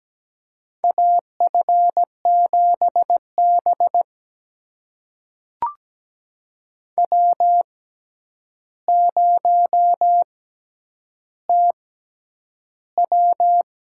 Morse code, heard as a keyed tone, about 17 wpm, spaced as for 9 wpm, 700 Hz. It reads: AF7B W0TW
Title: USA Call Signs - 6x Character Spacing - 17wpm